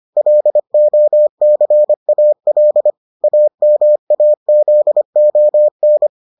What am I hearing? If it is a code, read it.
Morse code, heard as a keyed tone, about 25 wpm, 600 Hz. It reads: LOCAL AMAZON